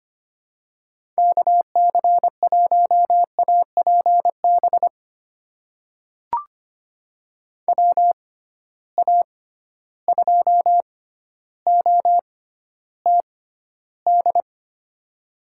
Morse code, heard as a keyed tone, about 25 wpm, spaced as for 13 wpm, 700 Hz. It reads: KC1APB WA2OTD
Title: USA Call Signs - 6 Characters - 6x Character Spacing - 25wpm